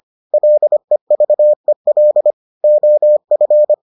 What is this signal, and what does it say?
Morse code, heard as a keyed tone, about 25 wpm, 600 Hz. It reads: LEVEL OF